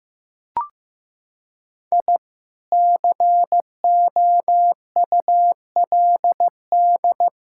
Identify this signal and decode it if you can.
Morse code, heard as a keyed tone, about 15 wpm, 700 Hz. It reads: I COULD
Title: Sentences from Top 1000 Words - Review Part 5 - 15wpm